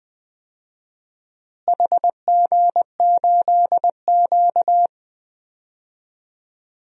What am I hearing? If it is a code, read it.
Morse code, heard as a keyed tone, about 20 wpm, 700 Hz. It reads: HG8Q
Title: DXCC-100 Call Signs - 4 Characters - Repeated 3x - 20wpm